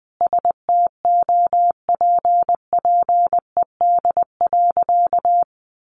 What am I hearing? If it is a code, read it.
Morse code, heard as a keyed tone, about 20 wpm, 700 Hz. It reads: STOPPED.
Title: A Sound Of Thunder 20wpm